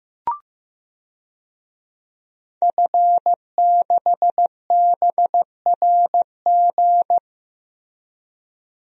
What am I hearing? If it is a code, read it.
Morse code, heard as a keyed tone, about 15 wpm, 700 Hz. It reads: F6BRG
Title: DXCC-100 Call Signs - 5 Characters - Repeated 2x - 15wpm